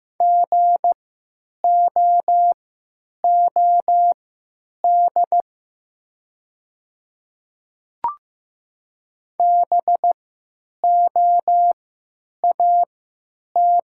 Morse code, heard as a keyed tone, about 15 wpm, spaced as for 9 wpm, 700 Hz. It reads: GOOD BOAT
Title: 3x Character Spacing - Top 500 Words - 15wpm